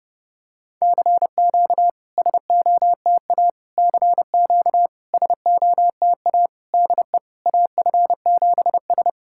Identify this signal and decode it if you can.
Morse code, heard as a keyed tone, about 30 wpm, 700 Hz. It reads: CQ SOTA CQ SOTA DE AF7S
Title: SOTA QSO Practice 30wpm